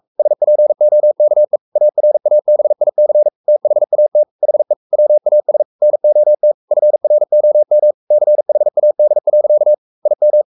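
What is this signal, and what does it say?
Morse code, heard as a keyed tone, about 44 wpm, 600 Hz. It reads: SPOKE ARABIC THAT HE WAS NOT FROM CHAD. IM
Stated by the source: Morse code at 43 wpm sending Ken Follett's novel Never in English, from CW med Gnister.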